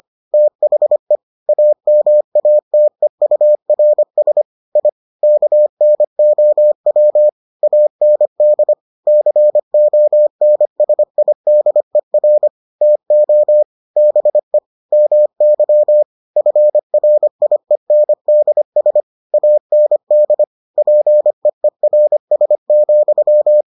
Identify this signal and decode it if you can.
Morse code, heard as a keyed tone, about 25 wpm, 600 Hz. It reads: THE AMATEURS I KNOW AND CONSIDER TO BE MY FRIENDS AND PEERS,